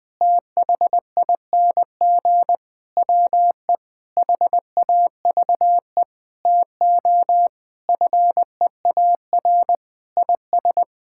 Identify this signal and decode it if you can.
Morse code, heard as a keyed tone, about 20 wpm, 700 Hz. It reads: THING WE HAVE TO FEAR IS
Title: Sets of 8 Words - 20wpm